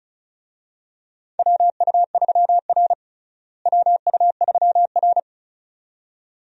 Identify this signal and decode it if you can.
Morse code, heard as a keyed tone, about 36 wpm, 700 Hz. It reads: WU3R WU3R
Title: USA Call Signs - 4 Characters - Repeated 2x - 35wpm